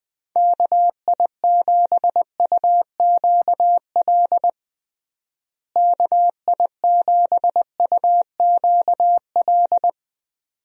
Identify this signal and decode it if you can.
Morse code, heard as a keyed tone, about 20 wpm, 700 Hz. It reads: KI7UQL KI7UQL
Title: USA Call Signs - 6 Characters - Repeated 2x - 20wpm